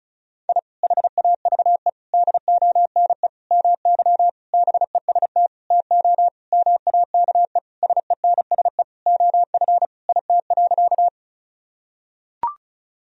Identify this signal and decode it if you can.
Morse code, heard as a keyed tone, about 36 wpm, 700 Hz. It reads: I HAVE DONE MY BEST TO MAKE SENSE OF IT.
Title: Sentences from Top 1000 Words - Review Part 4 - 35wpm